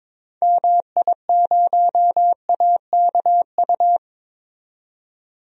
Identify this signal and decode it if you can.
Morse code, heard as a keyed tone, about 21 wpm, 700 Hz. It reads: MI0AKU
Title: DXCC-100 Call Signs - 6 Characters - 22wpm